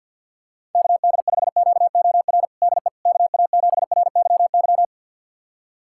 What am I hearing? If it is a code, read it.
Morse code, heard as a keyed tone, about 50 wpm, 700 Hz. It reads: KD5QOR DE KA7RYY
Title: USA Callsign DE Callsign - 6 Characters - 50wpm